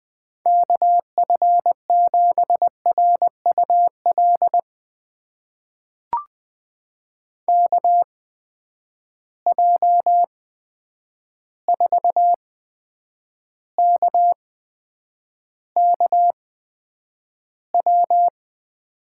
Morse code, heard as a keyed tone, about 20 wpm, spaced as for 9 wpm, 700 Hz. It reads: KF7RUL KJ4KKW
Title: USA Call Signs - 6 Characters - 8x Character Spacing - 20wpm